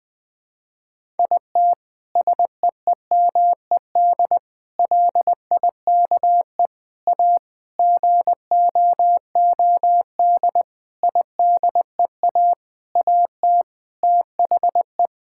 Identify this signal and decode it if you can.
Morse code, heard as a keyed tone, about 20 wpm, 700 Hz. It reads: IT SEEMED LIKE A GOOD IDEA AT THE
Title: Sentences with 9-Word Phrases - Spoken Beginning - 20wpm